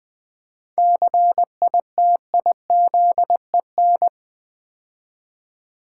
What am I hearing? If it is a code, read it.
Morse code, heard as a keyed tone, about 20 wpm, 700 Hz. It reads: CITIZEN
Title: 7 Letter Words - First 3 Letters Given - 20wpm